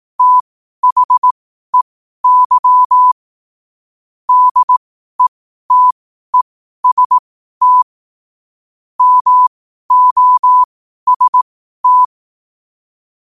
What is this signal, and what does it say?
Morse code, heard as a keyed tone, about 18 wpm, spaced as for 12 wpm, 1000 Hz. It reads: THEY DETEST MOST
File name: qotdfiles / 12wpm